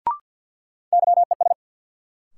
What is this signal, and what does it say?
Morse code, heard as a keyed tone, about 53 wpm, 700 Hz. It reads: YES
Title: Sentences from Top 500 Words - New Word Drill - 50wpm